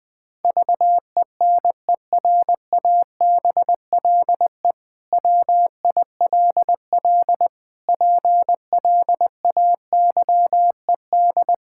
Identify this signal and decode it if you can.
Morse code, heard as a keyed tone, about 20 wpm, 700 Hz. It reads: VENERABLE WILL PLAYED
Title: Pangram Challenge v2 - 20wpm